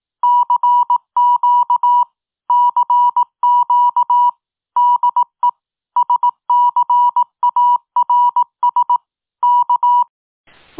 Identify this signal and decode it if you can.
Morse code, heard as a keyed tone, about 18 wpm, 970 Hz. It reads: CQ CQ DE SCARS K